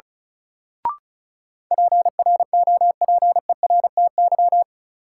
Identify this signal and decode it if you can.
Morse code, heard as a keyed tone, about 36 wpm, 700 Hz. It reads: PROPERTY
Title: Sentences from Top 1000 Words - New Word Drill - 35wpm